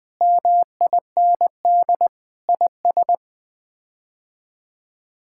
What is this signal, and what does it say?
Morse code, heard as a keyed tone, about 21 wpm, 700 Hz. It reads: MIND IS
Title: Sets of 8 Words - 20wpm